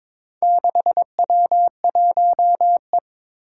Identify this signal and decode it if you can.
Morse code, heard as a keyed tone, about 22 wpm, 700 Hz. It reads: 6W1E
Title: Four Challenging Letter-Number Combinations 22wpm